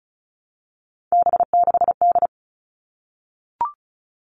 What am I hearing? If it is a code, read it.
Morse code, heard as a keyed tone, about 34 wpm, 700 Hz. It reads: B6D
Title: All Cut Numbers - 3-Digit 35wpm